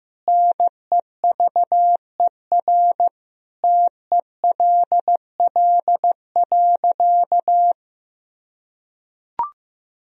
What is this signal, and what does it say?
Morse code, heard as a keyed tone, about 15 wpm, 700 Hz. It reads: NEVER TELL.
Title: Sentences from Top 300 Words and Abbreviations 15wpm